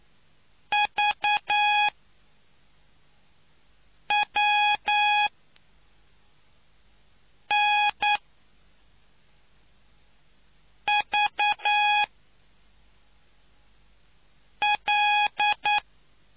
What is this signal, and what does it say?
Morse code, heard as a keyed tone, about 9 wpm, spaced as for 3 wpm, 830 Hz. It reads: VWNVL